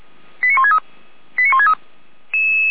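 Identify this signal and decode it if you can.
Five-tone selective call (ZVEI), heard as quick runs of five tones, tones of about 70 ms. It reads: 89163 89163
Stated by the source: ZVEI five-tone selective call received on an SDR, from the Feuerwehr Orlamünde fire brigade server.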